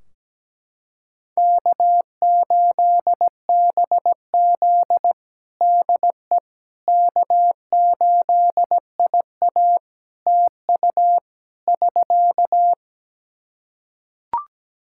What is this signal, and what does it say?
Morse code, heard as a keyed tone, about 17 wpm, 700 Hz. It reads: K8BZ DE K8IA TU <SK>
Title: K1USN SST QSO Practice 17wpm